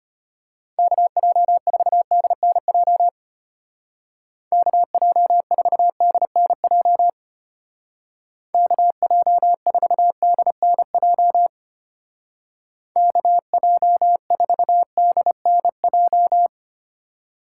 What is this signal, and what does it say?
Morse code, heard as a keyed tone, about 30 wpm, 700 Hz. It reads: KJ4DNJ KJ4DNJ KJ4DNJ KJ4DNJ
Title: Speed Racing - USA 2x3 Call Signs 25wpm